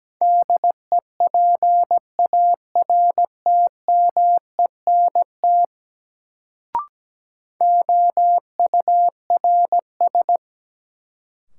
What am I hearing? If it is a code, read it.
Morse code, heard as a keyed tone, about 17 wpm, 700 Hz. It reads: DEPARTMENT OURS
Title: Sentences from Top 1000 Words - Review Part 5 - New Word Drill - 17wpm